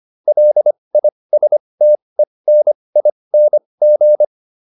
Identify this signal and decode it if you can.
Morse code, heard as a keyed tone, about 25 wpm, spaced as for 20 wpm, 600 Hz. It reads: LISTENING